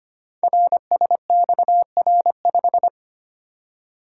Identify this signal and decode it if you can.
Morse code, heard as a keyed tone, about 25 wpm, 700 Hz. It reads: RSXR5
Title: Code Groups - 5 Letter:Numbers - 25wpm